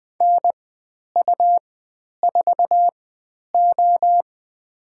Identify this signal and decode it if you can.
Morse code, heard as a keyed tone, about 20 wpm, spaced as for 10 wpm, 700 Hz. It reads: NU4O
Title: 102 Lesson 17g - Letter u - Call Signs 20wpm